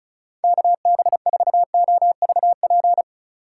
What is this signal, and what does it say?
Morse code, heard as a keyed tone, about 35 wpm, 690 Hz. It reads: KB4OVP
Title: USA Call Signs - 6 Characters - 3x Character Spacing - 35wpm